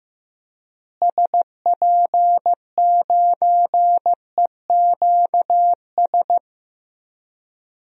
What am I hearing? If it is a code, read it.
Morse code, heard as a keyed tone, about 15 wpm, 700 Hz. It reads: SP9EQS
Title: DXCC-100 Call Signs - 6 Characters - Repeated 3x - 15wpm